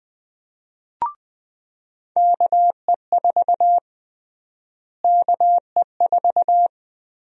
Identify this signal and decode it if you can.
Morse code, heard as a keyed tone, about 21 wpm, 700 Hz. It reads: KE4 KE4
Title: USA Call Sign Prefixes - Repeated 2x - 20wpm